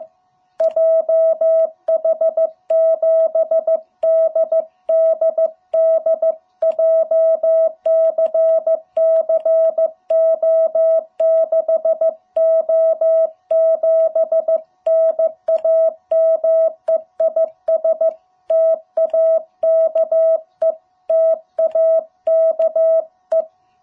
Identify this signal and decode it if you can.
Morse code, heard as a keyed tone, about 15 wpm, 660 Hz. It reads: JH7DDD JCCO6O7 NAME IS TAKE TAKE
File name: jh8cbh / 2021cw